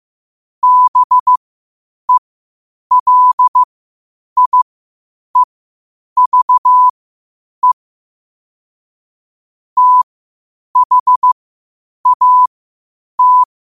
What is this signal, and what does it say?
Morse code, heard as a keyed tone, about 15 wpm, spaced as for 8 wpm, 1000 Hz. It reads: BELIEVE THAT
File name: qotdfiles / 8wpm